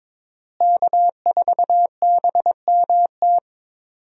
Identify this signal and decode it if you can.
Morse code, heard as a keyed tone, about 22 wpm, 700 Hz. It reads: K4BMT
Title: USA Call Signs - 5 Characters - Repeated 3x - 22wpm